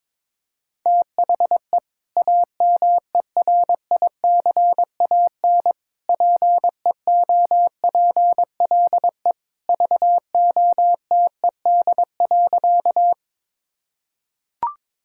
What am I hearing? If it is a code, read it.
Morse code, heard as a keyed tone, about 22 wpm, 700 Hz. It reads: THE AMERICAN PEOPLE VOTED.
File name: practice / 22wpm / Sentences with 3-Word Phrases - 22wpm